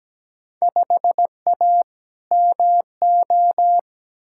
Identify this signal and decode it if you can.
Morse code, heard as a keyed tone, about 17 wpm, 700 Hz. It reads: 5A MO
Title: Field Day Exchanges Encore - 17wpm